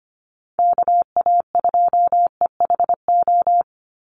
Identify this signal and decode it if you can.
Morse code, heard as a keyed tone, about 24 wpm, 700 Hz. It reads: KA2EHO
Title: USA Call Signs 25wpm